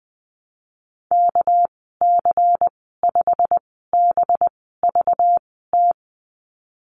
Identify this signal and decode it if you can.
Morse code, heard as a keyed tone, about 20 wpm, spaced as for 15 wpm, 700 Hz. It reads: KC5BVT
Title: 219 Lesson 35g - Number 0 - Call Signs 20wpm